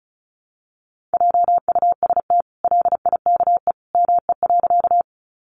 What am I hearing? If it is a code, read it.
Morse code, heard as a keyed tone, about 34 wpm, 700 Hz. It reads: JUST LIKE ME.
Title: Sentences from Top 200 Words 35wpm